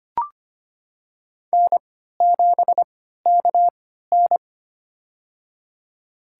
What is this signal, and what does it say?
Morse code, heard as a keyed tone, about 25 wpm, spaced as for 14 wpm, 700 Hz. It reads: N7KN
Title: USA Call Signs - 4 Characters - 3x Character Spacing - 25wpm